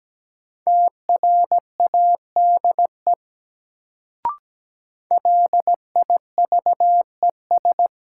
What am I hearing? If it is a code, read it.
Morse code, heard as a keyed tone, about 17 wpm, 700 Hz. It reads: TRADE LIVES
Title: Sets of 5 Words Encore - New Word Drill - 17wpm